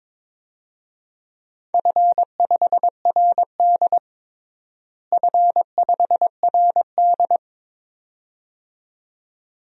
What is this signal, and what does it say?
Morse code, heard as a keyed tone, about 22 wpm, 700 Hz. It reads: F5RD F5RD
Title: DXCC-100 Call Signs - 4 Characters - Repeated 2x - 22wpm